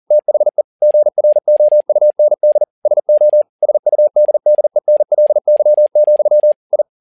Morse code, heard as a keyed tone, about 40 wpm, 600 Hz. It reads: THE GROUND SO SUDDENLY, I
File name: learning center / tower trouble / tower touble 40wpm